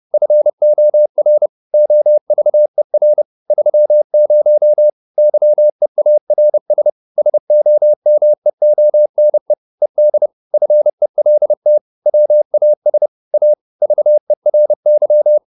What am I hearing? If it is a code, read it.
Morse code, heard as a keyed tone, about 30 wpm, 600 Hz. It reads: FOR OVER 30 YEARS SOMEONE ED FELT WAS A VERY